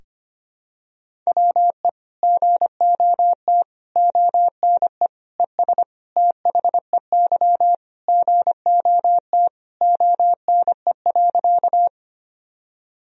Morse code, heard as a keyed tone, about 25 wpm, 700 Hz. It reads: WE GOT ONE ES THEY GOT ONE.
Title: Sentences from Top 300 Words and Abbreviations 25wpm